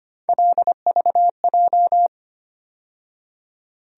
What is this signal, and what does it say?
Morse code, heard as a keyed tone, about 25 wpm, 700 Hz. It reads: LVJ
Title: USA Call Sign Postfixes - 3 Letters 25wpm